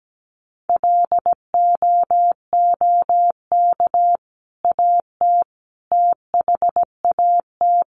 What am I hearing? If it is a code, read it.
Morse code, heard as a keyed tone, about 17 wpm, 700 Hz. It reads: LOOK AT THAT
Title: Sentences from Top 1100 Words 17wpm